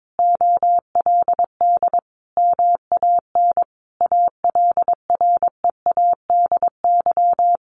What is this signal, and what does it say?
Morse code, heard as a keyed tone, about 22 wpm, 700 Hz. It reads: OLD MAN ALREADY
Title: Sentences from Top 700 Words 22wpm